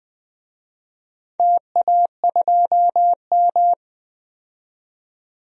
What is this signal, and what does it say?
Morse code, heard as a keyed tone, about 20 wpm, 700 Hz. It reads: TA2M
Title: DXCC-100 Call Signs - 4 Characters - 20wpm